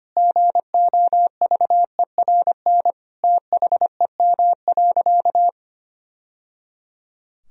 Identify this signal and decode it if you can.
Morse code, heard as a keyed tone, about 25 wpm, 700 Hz. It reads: GOVERN THEM.
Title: Sentences from Top 1000 Words - Review Part 2 - 25wpm